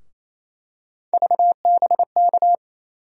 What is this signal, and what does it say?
Morse code, heard as a keyed tone, about 28 wpm, 700 Hz. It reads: VBK
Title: USA Call Sign Postfixes - 3 Letters 28wpm